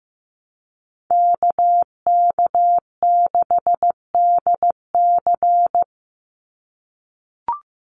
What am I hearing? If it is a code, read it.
Morse code, heard as a keyed tone, about 15 wpm, 700 Hz. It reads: KK6DC
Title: USA Call Signs 15wpm